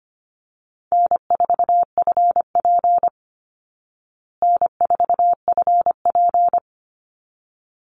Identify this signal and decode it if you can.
Morse code, heard as a keyed tone, about 25 wpm, 700 Hz. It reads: N4FP N4FP
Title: Easy Call Signs Repeated 2x - 25wpm